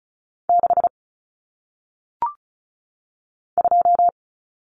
Morse code, heard as a keyed tone, about 36 wpm, 700 Hz. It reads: 6 2
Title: Numbers - 1-Digit 35wpm